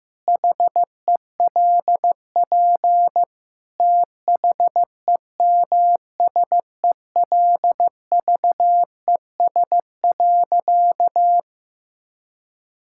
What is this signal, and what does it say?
Morse code, heard as a keyed tone, about 15 wpm, 700 Hz. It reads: HELP THEMSELVES.
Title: Sentences from Top 1000 Words - Review Part 5 - 15wpm